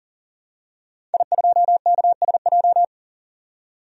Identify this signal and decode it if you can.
Morse code, heard as a keyed tone, about 40 wpm, 700 Hz. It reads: I2KSJ
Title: DXCC-100 Call Signs - 5 Characters - 40wpm